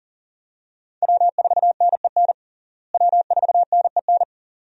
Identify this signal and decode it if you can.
Morse code, heard as a keyed tone, about 40 wpm, 700 Hz. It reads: W4NEN W4NEN
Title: USA Call Signs - 5 Characters - Repeated 2x - 40wpm